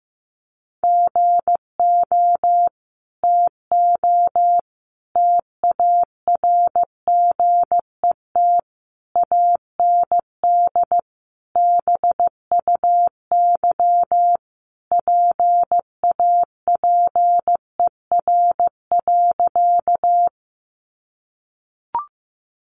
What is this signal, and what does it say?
Morse code, heard as a keyed tone, about 15 wpm, 700 Hz. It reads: GO TO TARGET AND BUY PAPER.
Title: Sentences from Top 1200 Words 15wpm